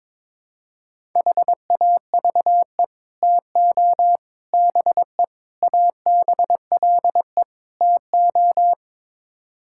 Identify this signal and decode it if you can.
Morse code, heard as a keyed tone, about 22 wpm, 700 Hz. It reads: HAVE TO BE ABLE TO
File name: practice / 22wpm / Build A Phrase - 5-Word Phrases - 22wpm